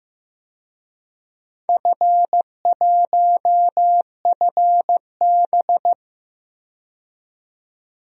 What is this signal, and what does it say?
Morse code, heard as a keyed tone, about 15 wpm, 700 Hz. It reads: F1FB
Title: DXCC-100 Call Signs - 4 Characters - 15wpm